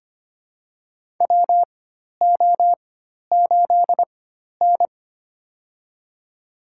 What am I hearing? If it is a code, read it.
Morse code, heard as a keyed tone, about 25 wpm, spaced as for 12 wpm, 700 Hz. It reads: WO8N
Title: USA Call Signs - 4 Characters - 4x Character Spacing - 25wpm